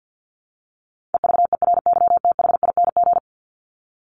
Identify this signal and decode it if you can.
Morse code, heard as a keyed tone, about 50 wpm, 700 Hz. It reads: EVERYTHING
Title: Top 501-750 Words 50wpm